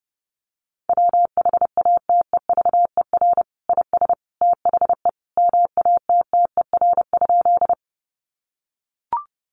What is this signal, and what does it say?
Morse code, heard as a keyed tone, about 30 wpm, 700 Hz. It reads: WHATEVER IS THE MATTER?
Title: Sentences from Top 1100 Words 30wpm